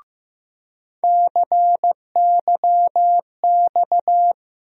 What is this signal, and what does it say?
Morse code, heard as a keyed tone, about 15 wpm, 700 Hz. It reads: CYX
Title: USA Call Sign Postfixes - 3 Letters 15wpm